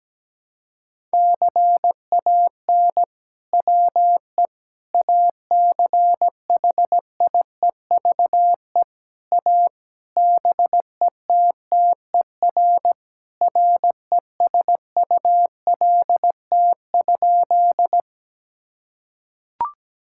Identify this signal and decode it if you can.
Morse code, heard as a keyed tone, about 17 wpm, 700 Hz. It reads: CAN WE ACHIEVE A BETTER RESULT?
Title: Sentences from Top 1000 Words - Review Part 6 - 17wpm